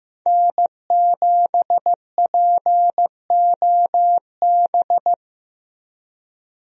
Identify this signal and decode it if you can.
Morse code, heard as a keyed tone, about 15 wpm, 700 Hz. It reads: N7POB
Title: USA Call Signs - 5 Characters - 15wpm